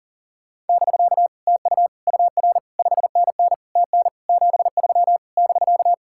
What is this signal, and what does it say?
Morse code, heard as a keyed tone, about 50 wpm, 700 Hz. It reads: <BK> TU UR 5NN TN 73 <BK>
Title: Speed-Racing Common QSO Phrases - 40wpm